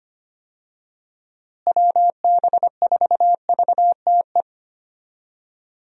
Morse code, heard as a keyed tone, about 25 wpm, 700 Hz. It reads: WB4VTE
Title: USA Call Signs - 6 Characters - Repeated 3x - 25wpm